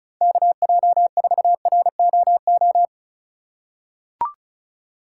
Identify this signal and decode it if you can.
Morse code, heard as a keyed tone, about 36 wpm, 700 Hz. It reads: KJ4ROO
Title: USA Call Signs - 6 Characters - 4x Character Spacing - 35wpm